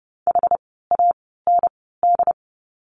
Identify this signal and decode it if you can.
Morse code, heard as a keyed tone, about 30 wpm, spaced as for 17 wpm, 700 Hz. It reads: HAND
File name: course / 30:10wpm / 108 Lesson 18f - Letter m - Words 30wpm